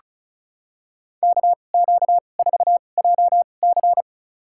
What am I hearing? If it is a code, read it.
Morse code, heard as a keyed tone, about 36 wpm, spaced as for 25 wpm, 700 Hz. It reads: KQ4JC
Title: USA Call Signs - 5 Characters - 2x Character Spacing - 35wpm